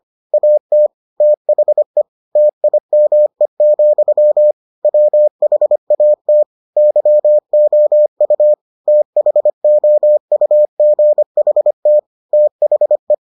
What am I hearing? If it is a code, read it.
Morse code, heard as a keyed tone, about 25 wpm, 600 Hz. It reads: AT THE TIME, WHAT YOU THOUGHT THE